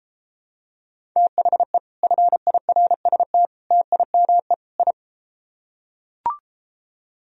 Morse code, heard as a keyed tone, about 34 wpm, 700 Hz. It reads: THE FIRST TIME I